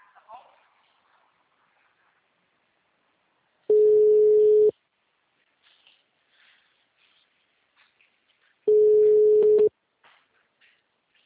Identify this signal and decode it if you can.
Ringback tone (Europe, 425 Hz), with speech.